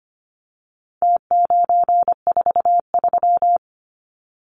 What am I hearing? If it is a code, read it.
Morse code, heard as a keyed tone, about 25 wpm, 700 Hz. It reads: T943
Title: Numbers - Contest Style - 4-Digit 25wpm